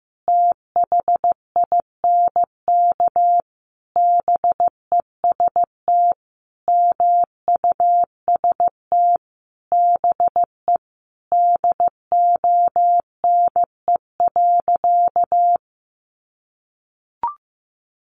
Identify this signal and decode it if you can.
Morse code, heard as a keyed tone, about 15 wpm, 700 Hz. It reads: THINK BEST MUST BE DONE.
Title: Sentences from Top 1100 Words 15wpm